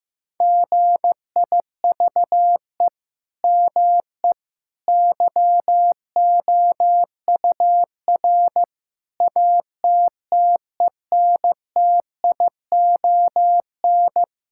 Morse code, heard as a keyed tone, about 15 wpm, 700 Hz. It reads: GIVE ME YOUR ATTENTION
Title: Sentences from Top 1000 Words - Review Part 6 - 15wpm